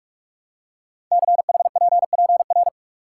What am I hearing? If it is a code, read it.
Morse code, heard as a keyed tone, about 44 wpm, 690 Hz. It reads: CHPPR
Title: Code Groups - 5 Letters - 45wpm